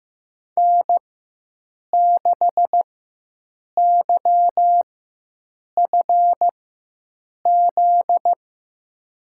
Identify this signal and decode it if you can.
Morse code, heard as a keyed tone, about 15 wpm, spaced as for 7 wpm, 700 Hz. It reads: N6YFZ
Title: USA Call Signs - 5 Characters - 4x Character Spacing - 15wpm